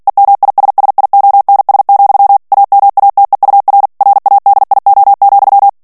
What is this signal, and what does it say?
Morse code, heard as a keyed tone, about 48 wpm, 800 Hz. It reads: EMISSIONS, AMATEUR RADIO,
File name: ANGLAIS texte clair 3 vitesse 45 mots minute